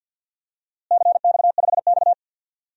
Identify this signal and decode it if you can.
Morse code, heard as a keyed tone, about 51 wpm, 700 Hz. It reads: CX5X
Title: DXCC-100 Call Signs - 4 Characters - Repeated 2x - 50wpm